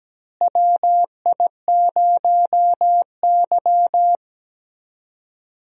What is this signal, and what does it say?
Morse code, heard as a keyed tone, about 17 wpm, 700 Hz. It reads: WI0Y
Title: USA Call Signs - 4 Characters - Repeated 2x - 17wpm